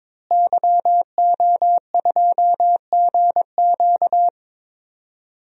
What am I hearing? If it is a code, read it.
Morse code, heard as a keyed tone, about 22 wpm, 700 Hz. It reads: YO2GQ
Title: DXCC-100 Call Signs - 5 Characters - Repeated 2x - 22wpm